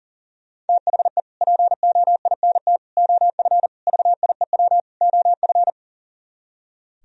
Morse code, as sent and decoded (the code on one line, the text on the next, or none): - .... / .--. --- .. -. - / --- ..-. / ...- .. .-- / --- ..-.
THE POINT OF VIEW OF